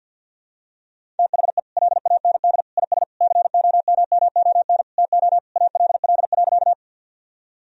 Morse code, heard as a keyed tone, about 52 wpm, 700 Hz. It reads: THE LAND IS COMMON TO ALL.
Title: Sentences from Top 1000 Words - Review Part 3 - 50wpm